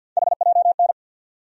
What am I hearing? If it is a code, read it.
Morse code, heard as a keyed tone, about 53 wpm, 700 Hz. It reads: HJN
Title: USA Call Sign Postfixes - 3 Letters 50wpm